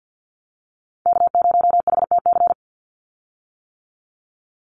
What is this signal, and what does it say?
Morse code, heard as a keyed tone, about 50 wpm, 700 Hz. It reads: K0HTC
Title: USA Call Signs 50wpm